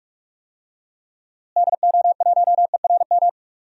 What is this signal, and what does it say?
Morse code, heard as a keyed tone, about 46 wpm, 700 Hz. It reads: DO1ERM
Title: DXCC-100 Call Signs - 6 Characters - Repeated 3x - 45wpm